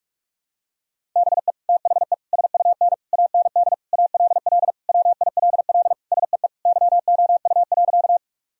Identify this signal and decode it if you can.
Morse code, heard as a keyed tone, about 46 wpm, 700 Hz. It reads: BE THE SUN AND ALL WILL SEE YOU.
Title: Sentences from Top 1000 Words - Review Part 1 - 45wpm